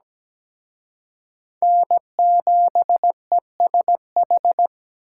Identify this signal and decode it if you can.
Morse code, heard as a keyed tone, about 20 wpm, 700 Hz. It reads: N7ESH
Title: Speed Racing - USA Call Signs 17wpm